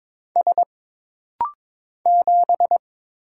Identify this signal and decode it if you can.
Morse code, heard as a keyed tone, about 22 wpm, 700 Hz. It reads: S 7
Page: This rapid-fire timing is geared to push your ICR (Instant Character Recognition) skills.